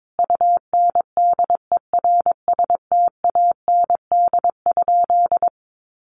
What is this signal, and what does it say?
Morse code, heard as a keyed tone, about 22 wpm, 700 Hz. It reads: UNDERSTAND?
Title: Sentences from Top 1500 Words 22wpm